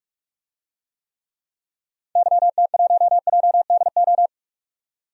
Morse code, heard as a keyed tone, about 44 wpm, 700 Hz. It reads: YT1JDO
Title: DXCC-100 Call Signs - 6 Characters - 45wpm